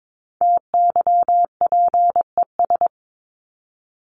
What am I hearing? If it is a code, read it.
Morse code, heard as a keyed tone, about 22 wpm, 700 Hz. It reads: TYPES
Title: ES:S Suffix - Top 100 Words with Base Words 22wpm